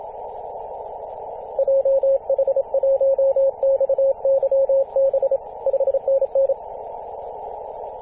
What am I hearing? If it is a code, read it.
Morse code, heard as a keyed tone, about 26 wpm, 540 Hz. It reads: JH1XYB 5NN
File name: posts / 2016-arp-15-2115z-vk9l-g7vjr-18-081mhz-cw